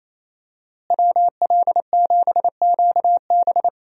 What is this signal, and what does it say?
Morse code, heard as a keyed tone, about 27 wpm, 700 Hz. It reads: WL7QB
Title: USA Call Signs - 5 Characters - Repeated 2x - 28wpm